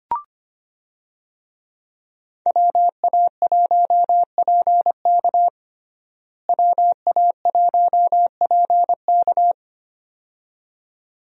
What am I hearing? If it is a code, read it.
Morse code, heard as a keyed tone, about 25 wpm, 700 Hz. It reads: WA1PK WA1PK
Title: USA Call Signs - 5 Characters - Repeated 2x - 25wpm